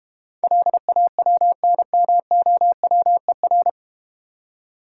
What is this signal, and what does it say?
Morse code, heard as a keyed tone, about 32 wpm, 700 Hz. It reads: LAWNMOWER